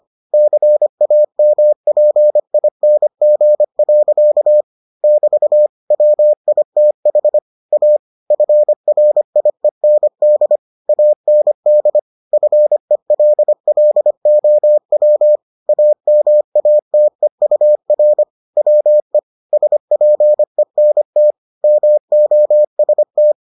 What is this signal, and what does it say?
Morse code, heard as a keyed tone, about 25 wpm, 600 Hz. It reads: CAMPING. <BT> WITH A FRIEND AND FELLOW AMATEUR WE SPENT MOST